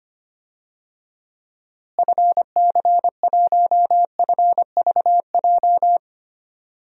Morse code, heard as a keyed tone, about 25 wpm, 700 Hz. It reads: FC1FVJ
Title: DXCC-100 Call Signs - 6 Characters - Repeated 2x - 25wpm